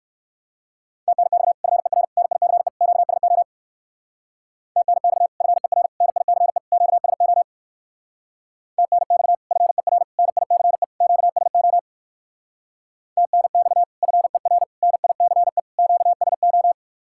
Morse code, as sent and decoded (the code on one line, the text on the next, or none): - -. -..- / ..-. .-. / -. .. -.-. / --.- ... --- / - -. -..- / ..-. .-. / -. .. -.-. / --.- ... --- / - -. -..- / ..-. .-. / -. .. -.-. / --.- ... --- / - -. -..- / ..-. .-. / -. .. -.-. / --.- ... ---
TNX FER NICE QSO TNX FER NICE QSO TNX FER NICE QSO TNX FER NICE QSO